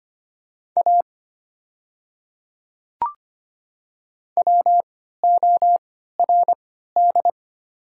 Morse code, heard as a keyed tone, about 25 wpm, spaced as for 14 wpm, 700 Hz. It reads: A WORD
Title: 3x Character Spacing - Top 100 Words - 25wpm